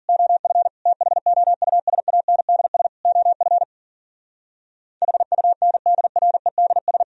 Morse code, heard as a keyed tone, about 52 wpm, 700 Hz. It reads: OF THOUSANDS OF HUNDREDS